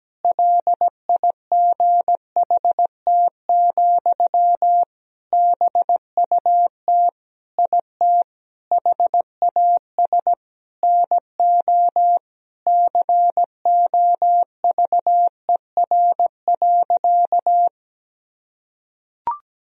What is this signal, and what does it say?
Morse code, heard as a keyed tone, about 17 wpm, 700 Hz. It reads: LIGHT, BUT IT HAS NO COVER.
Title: Sentences from Top 1000 Words - Review Part 1 - 17wpm